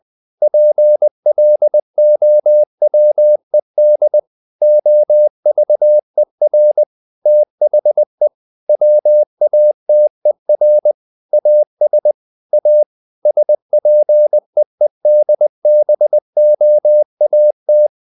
Morse code, heard as a keyed tone, about 20 wpm, 600 Hz. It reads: PLOWED OVER THE WATER AS A SPEEDBOAT
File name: the old man and the sea part 2 20wpm